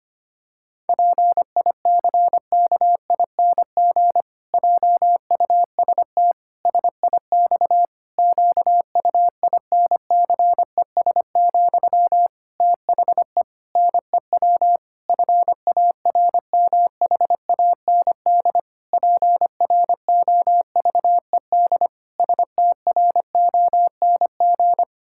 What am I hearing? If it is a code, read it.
Morse code, heard as a keyed tone, about 25 wpm, 700 Hz. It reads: PICKING JUST SIX QUINCES, THE NEW FARMHAND PROVED STRONG
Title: Pangram Challenge v2 - 25wpm